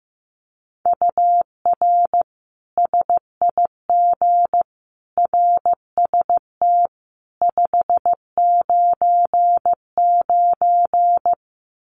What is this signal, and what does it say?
Morse code, heard as a keyed tone, about 15 wpm, 700 Hz. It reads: UR SIG RST 599
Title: Common QSO Phrases 15wpm